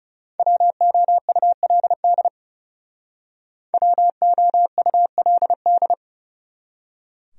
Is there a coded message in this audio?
Yes: Morse code from 0.4 to 5.9 s, heard as a keyed tone, about 40 wpm, 700 Hz. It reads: WOULD WOULD